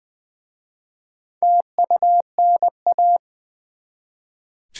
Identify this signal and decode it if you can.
Morse code, heard as a keyed tone, about 20 wpm, 690 Hz. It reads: TUNA
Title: Easy Words with Hints - 20wpm